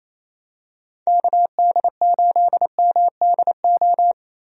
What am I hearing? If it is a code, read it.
Morse code, heard as a keyed tone, about 28 wpm, 700 Hz. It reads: KD8MDO
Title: USA Call Signs - 6 Characters - 28wpm